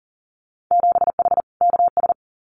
Morse code, heard as a keyed tone, about 38 wpm, 700 Hz. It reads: 7H KS
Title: Winter Field Day Exchanges Encore 40wpm